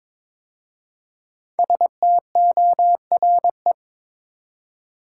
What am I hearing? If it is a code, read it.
Morse code, heard as a keyed tone, about 22 wpm, 690 Hz. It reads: STORE
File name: 5 Letter Words - First 3 Letters Given - 22wpm